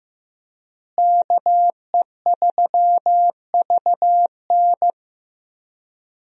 Morse code, heard as a keyed tone, about 15 wpm, 700 Hz. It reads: KE3VN
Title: USA Call Signs - 5 Characters - Repeated 3x - 15wpm